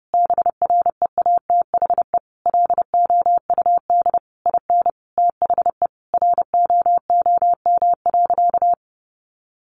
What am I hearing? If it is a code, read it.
Morse code, heard as a keyed tone, about 30 wpm, 700 Hz. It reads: BREATHE LOUD IN THE ROOM.
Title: A Sound Of Thunder 30wpm